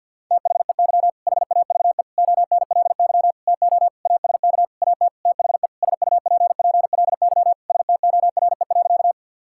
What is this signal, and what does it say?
Morse code, heard as a keyed tone, about 50 wpm, 700 Hz. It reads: THEY HAVE ONLY TO ASK AT THE SUPPLY STORE.
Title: Sentences from Top 1000 Words - Review Part 5 - 50wpm